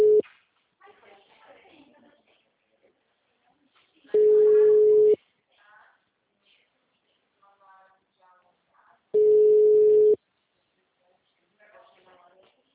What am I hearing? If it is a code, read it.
Ringback tone (Europe, 425 Hz), with speech.
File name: out- -2003- - - .138